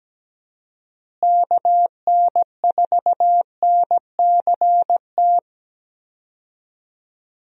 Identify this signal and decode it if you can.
Morse code, heard as a keyed tone, about 17 wpm, 700 Hz. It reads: KN4NCT
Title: USA Call Signs - 6 Characters - Repeated 2x - 17wpm